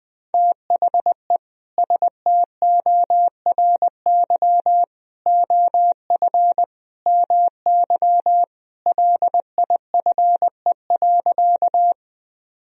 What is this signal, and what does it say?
Morse code, heard as a keyed tone, about 20 wpm, 700 Hz. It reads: THE STORY OF MY LIFE.
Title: Sentences from Top 1000 Words - Review Part 1 - 20wpm